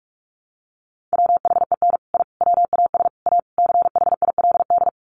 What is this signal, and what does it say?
Morse code, heard as a keyed tone, about 44 wpm, 700 Hz. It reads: WHEN I WAS A CHILD
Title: Sets of 5 Words Encore 45wpm